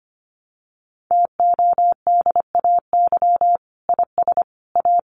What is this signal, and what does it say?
Morse code, heard as a keyed tone, about 25 wpm, 700 Hz. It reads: TODAY IS A
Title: Sentences from Top 600 Words 25wpm